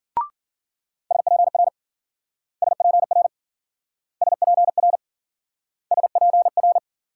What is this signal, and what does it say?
Morse code, heard as a keyed tone, about 54 wpm, 700 Hz. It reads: SPR SPR SPR SPR